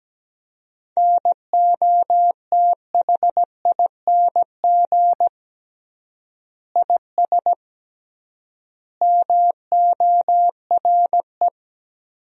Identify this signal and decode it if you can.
Morse code, heard as a keyed tone, about 17 wpm, 700 Hz. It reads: NOTHING IS MORE
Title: 3x Word-Spacing - Sentences from Top 500 Words - 17wpm